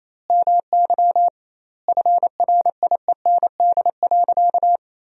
Morse code, heard as a keyed tone, about 28 wpm, 700 Hz. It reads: MY FRIEND.
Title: 2x Word-Spacing - Sentences from Top 500 Words - 28wpm